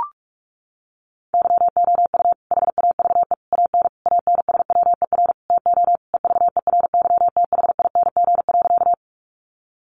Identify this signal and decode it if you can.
Morse code, heard as a keyed tone, about 44 wpm, 700 Hz. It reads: YOU HAVE AN ANSWER TO EVERYTHING.
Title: Sentences from Top 700 Words 45wpm